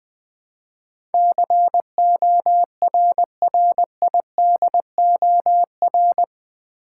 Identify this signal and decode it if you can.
Morse code, heard as a keyed tone, about 20 wpm, 700 Hz. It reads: CORRIDOR
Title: Build A Word - 3 Letters At A Time - 8-Letter Words - 20wpm